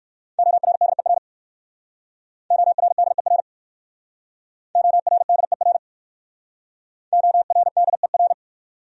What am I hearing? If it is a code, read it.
Morse code, heard as a keyed tone, about 55 wpm, 700 Hz. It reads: ORDER ORDER ORDER ORDER
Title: Speed-Racing Top 500 Words 45wpm